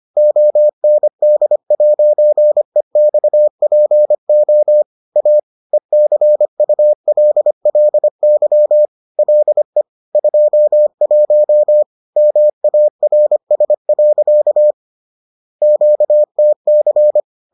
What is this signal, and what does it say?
Morse code, heard as a keyed tone, about 25 wpm, 600 Hz. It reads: OND'EXPO A ECULLY LE 21 MARS. QTC
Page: bulletin UFT sous forme texte et en CW